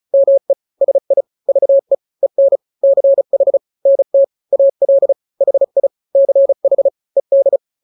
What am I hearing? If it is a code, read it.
Morse code, heard as a keyed tone, about 35 wpm, 550 Hz. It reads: ME SI VE EN CH NT AL HI CH ED
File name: n-grams / 2-grams 35 30 WPM